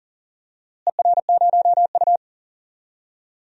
Morse code, heard as a keyed tone, about 40 wpm, 700 Hz. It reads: ER0U
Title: DXCC-100 Call Signs - 4 Characters - Repeated 2x - 40wpm